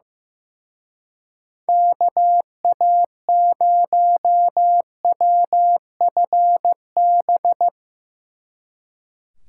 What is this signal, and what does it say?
Morse code, heard as a keyed tone, about 16 wpm, 700 Hz. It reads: KA0WFB
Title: Speed Racing - USA 2x3 Call Signs 15wpm